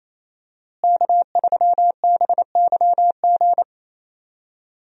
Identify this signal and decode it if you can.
Morse code, heard as a keyed tone, about 27 wpm, 700 Hz. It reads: K3BYG
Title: USA Call Signs - 5 Characters - Repeated 3x - 28wpm